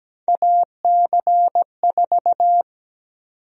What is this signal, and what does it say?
Morse code, heard as a keyed tone, about 17 wpm, 700 Hz. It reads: AC4
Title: USA Call Sign Prefixes - 17wpm